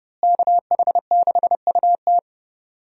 Morse code, heard as a keyed tone, about 30 wpm, 700 Hz. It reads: KH6UT
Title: USA Call Signs - 5 Characters - Repeated 2x - 30wpm